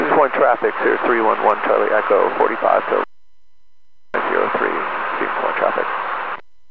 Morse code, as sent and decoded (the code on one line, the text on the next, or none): none